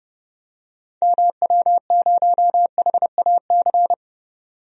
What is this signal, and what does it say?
Morse code, heard as a keyed tone, about 30 wpm, 700 Hz. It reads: MW0HAC
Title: DXCC-100 Call Signs - 6 Characters - 30wpm